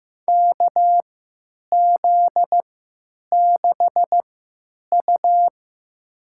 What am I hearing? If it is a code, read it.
Morse code, heard as a keyed tone, about 15 wpm, spaced as for 9 wpm, 700 Hz. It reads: KZ6U
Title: USA Call Signs - 3x Character Spacing - 15wpm